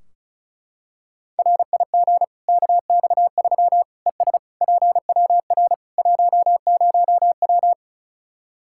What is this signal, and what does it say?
Morse code, heard as a keyed tone, about 36 wpm, 700 Hz. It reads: RIG KX3 ES PWR 10W
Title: Practice QSO - Exchange 2 - Info - 35wpm